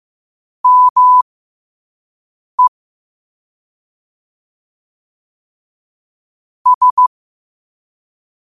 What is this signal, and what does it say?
Morse code, heard as a keyed tone, about 15 wpm, spaced as for 5 wpm, 1000 Hz. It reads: ME S
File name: qotdfiles / 5wpm